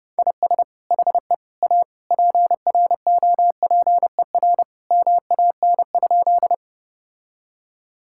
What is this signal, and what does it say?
Morse code, heard as a keyed tone, about 30 wpm, 700 Hz. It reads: IS HE A PROPER MAN?
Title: Sentences from Top 1000 Words - Review Part 5 - 30wpm